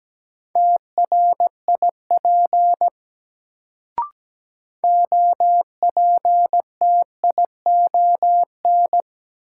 Morse code, heard as a keyed tone, about 17 wpm, 700 Hz. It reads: TRIP OPTION
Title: Sentences from Top 1300 Words - New Word Drill Alt - 17wpm